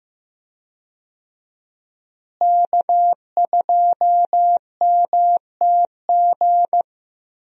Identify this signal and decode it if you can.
Morse code, heard as a keyed tone, about 15 wpm, 700 Hz. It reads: K2MTG
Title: USA Call Signs - 5 Characters - Repeated 3x - 15wpm